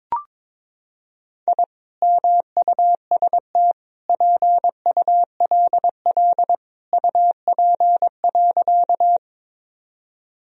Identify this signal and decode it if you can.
Morse code, heard as a keyed tone, about 22 wpm, 700 Hz. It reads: I MUST PULL UP.
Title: Sentences from Top 1000 Words - Review Part 2 - 22wpm